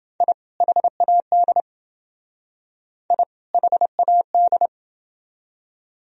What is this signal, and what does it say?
Morse code, heard as a keyed tone, about 28 wpm, 700 Hz. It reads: I HAD I HAD